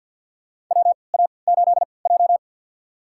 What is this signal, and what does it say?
Morse code, heard as a keyed tone, about 53 wpm, spaced as for 29 wpm, 700 Hz. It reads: WA8J
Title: USA Call Signs - 4 Characters - 3x Character Spacing - 50wpm